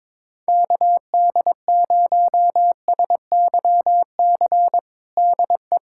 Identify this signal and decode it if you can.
Morse code, heard as a keyed tone, about 21 wpm, 700 Hz. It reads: KD0SYC DE
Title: USA Callsign DE Callsign 22wpm